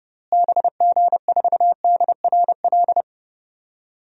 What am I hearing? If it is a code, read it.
Morse code, heard as a keyed tone, about 32 wpm, 700 Hz. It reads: BG4DRL
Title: DXCC-100 Call Signs - 6 Characters - Repeated 3x - 30wpm